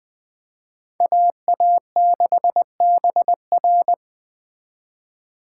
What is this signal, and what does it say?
Morse code, heard as a keyed tone, about 20 wpm, 700 Hz. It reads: AA6BR
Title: Speed Racing - USA 2x2 Call Signs 17wpm